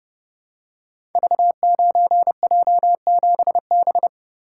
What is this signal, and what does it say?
Morse code, heard as a keyed tone, about 31 wpm, 700 Hz. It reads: V9J7B
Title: Code Groups - 5 Letter:Numbers - 30wpm